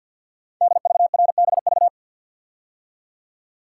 Morse code, heard as a keyed tone, about 47 wpm, 700 Hz. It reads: DVRBV
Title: Code Groups - 5 Letters - 50wpm